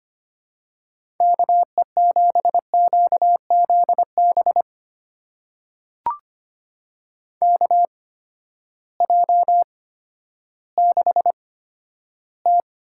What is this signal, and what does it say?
Morse code, heard as a keyed tone, about 25 wpm, spaced as for 12 wpm, 700 Hz. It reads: KE7QZB KJ6T
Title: USA Call Signs - 6 Characters - 8x Character Spacing - 25wpm